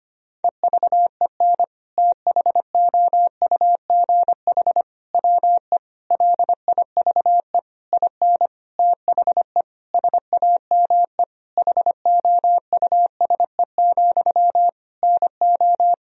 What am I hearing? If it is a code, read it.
Morse code, heard as a keyed tone, about 25 wpm, 700 Hz. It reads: EVEN THOUGH WE LIVE IN THE SAME HOUSE, NO
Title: Sentences from Top 1000 Words - Review Part 2 - 25wpm